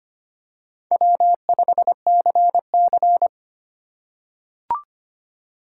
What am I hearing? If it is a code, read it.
Morse code, heard as a keyed tone, about 25 wpm, 700 Hz. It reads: W5CC
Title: USA Call Signs - 4 Characters - 3x Character Spacing - 25wpm